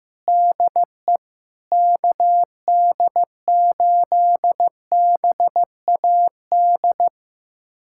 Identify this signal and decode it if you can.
Morse code, heard as a keyed tone, about 16 wpm, 700 Hz. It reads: DE KD8BAD
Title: USA Callsign DE Callsign - 6 Characters - 15wpm